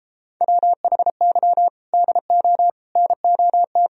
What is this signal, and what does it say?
Morse code, heard as a keyed tone, about 33 wpm, 700 Hz. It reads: WHY DO NOT